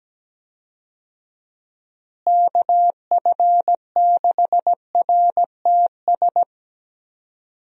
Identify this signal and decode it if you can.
Morse code, heard as a keyed tone, about 17 wpm, 700 Hz. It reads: KF6RTS
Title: USA Call Signs - 6 Characters - Repeated 2x - 17wpm